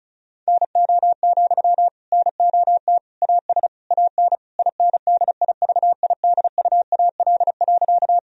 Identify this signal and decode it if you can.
Morse code, heard as a keyed tone, about 36 wpm, 700 Hz. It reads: NO, NOT AS AN INDIVIDUAL.
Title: Sentences from Top 1000 Words - Review Part 4 - 35wpm